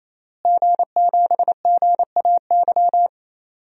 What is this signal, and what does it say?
Morse code, heard as a keyed tone, about 28 wpm, 700 Hz. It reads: G7GAY
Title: DXCC-100 Call Signs - 5 Characters - Repeated 3x - 28wpm